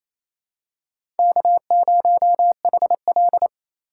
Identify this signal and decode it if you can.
Morse code, heard as a keyed tone, about 28 wpm, 700 Hz. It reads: K0HL
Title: USA Call Signs - 4 Characters - 6x Character Spacing - 28wpm